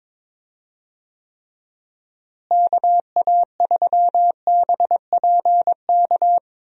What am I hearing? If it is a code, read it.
Morse code, heard as a keyed tone, about 22 wpm, 700 Hz. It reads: KA3BPK
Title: USA Call Signs - 6 Characters - Repeated 3x - 22wpm